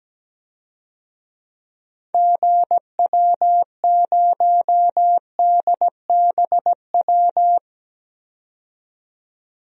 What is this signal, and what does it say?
Morse code, heard as a keyed tone, about 17 wpm, 700 Hz. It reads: GW0DBW
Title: DXCC-100 Call Signs - 6 Characters - 17wpm